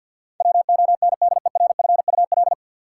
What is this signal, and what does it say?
Morse code, heard as a keyed tone, about 51 wpm, 700 Hz. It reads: WONDERFUL
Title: Sentences from Top 1100 Words - New Word Drill - 50wpm